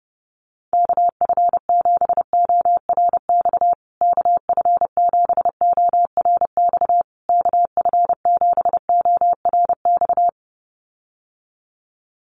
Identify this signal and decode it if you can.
Morse code, heard as a keyed tone, about 30 wpm, 700 Hz. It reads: KF7ORX KF7ORX KF7ORX
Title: USA Call Signs Repeated 3x 30wpm